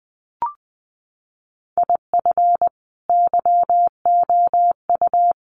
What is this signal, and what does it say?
Morse code, heard as a keyed tone, about 20 wpm, 700 Hz. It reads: IF YOU
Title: Sets of 5 Words 20wpm